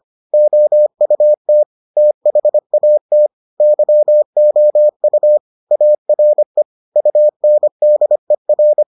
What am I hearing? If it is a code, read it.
Morse code, heard as a keyed tone, about 25 wpm, 600 Hz. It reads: OUT THAT YOU ARE UNDER